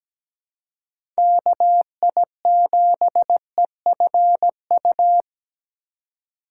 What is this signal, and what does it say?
Morse code, heard as a keyed tone, about 17 wpm, 700 Hz. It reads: KI7EFU
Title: USA Call Signs - 3x Character Spacing - 17wpm